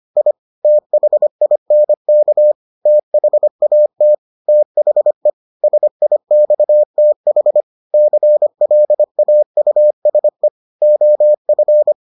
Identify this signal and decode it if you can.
Morse code, heard as a keyed tone, about 25 wpm, 600 Hz. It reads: I THINK THAT THE SIXTH CLAUSE OF